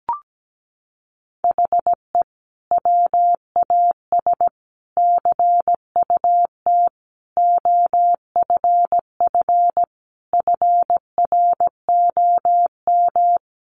Morse code, heard as a keyed tone, about 17 wpm, 700 Hz. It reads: HE WAS CUT OFF FROM
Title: Sentences from Top 1100 Words 17wpm